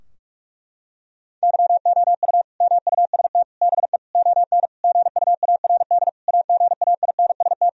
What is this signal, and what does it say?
Morse code, heard as a keyed tone, about 47 wpm, 700 Hz. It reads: YOU MUST BE ON GUARD AGAINST
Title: Sentences from Top 1000 Words - Review Part 4 - 45wpm